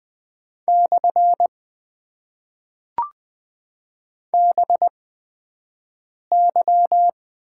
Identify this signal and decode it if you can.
Morse code, heard as a keyed tone, about 20 wpm, spaced as for 5 wpm, 700 Hz. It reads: / BY
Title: Top 90 QSO Elements - 8x Character Spacing - 20wpm